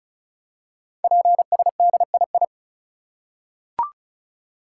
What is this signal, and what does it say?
Morse code, heard as a keyed tone, about 36 wpm, 700 Hz. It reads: PSDII
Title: Code Groups - 5 Letter:Numbers - 35wpm